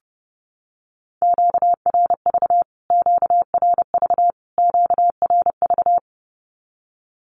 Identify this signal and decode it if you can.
Morse code, heard as a keyed tone, about 30 wpm, 700 Hz. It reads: QRV QRV QRV
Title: Q-Codes Repeated 3x 30wpm